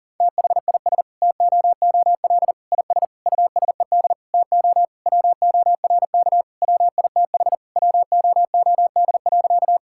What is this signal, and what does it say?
Morse code, heard as a keyed tone, about 40 wpm, 700 Hz. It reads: THIS TOOL IS USED TO WORK WITH WOOD.
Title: Sentences from Top 1000 Words - Review Part 6 - 40wpm